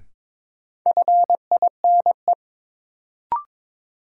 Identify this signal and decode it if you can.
Morse code, heard as a keyed tone, about 21 wpm, 700 Hz. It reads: FINE
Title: Sentences from Top 500 Words - New Word Drill - 22wpm